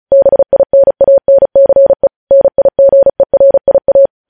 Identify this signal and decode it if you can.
Morse code, heard as a keyed tone, about 36 wpm, 550 Hz. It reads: BINANCE NIGERIA
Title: News Headlines at 35